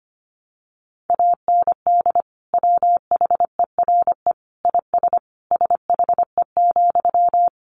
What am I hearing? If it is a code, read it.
Morse code, heard as a keyed tone, about 25 wpm, 700 Hz. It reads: AND WHERE IS SHE,
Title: Sentences from Top 200 Words 25wpm